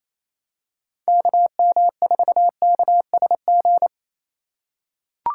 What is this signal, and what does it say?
Morse code, heard as a keyed tone, about 28 wpm, 700 Hz. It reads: KM4KSG
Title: USA Call Signs - 2x Character Spacing - 28wpm